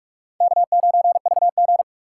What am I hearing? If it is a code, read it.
Morse code, heard as a keyed tone, about 67 wpm, spaced as for 56 wpm, 700 Hz. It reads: K9VG
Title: Speed Racing - USA 1x2 Call Signs 45wpm